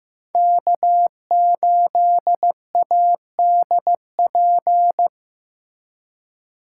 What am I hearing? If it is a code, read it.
Morse code, heard as a keyed tone, about 18 wpm, 700 Hz. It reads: K8ADP
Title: Speed Racing - USA 1x3 Call Signs 15wpm